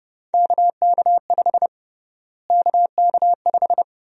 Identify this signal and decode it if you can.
Morse code, heard as a keyed tone, about 32 wpm, 700 Hz. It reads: KK5 KK5
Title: USA Call Sign Prefixes - Repeated 2x - 30wpm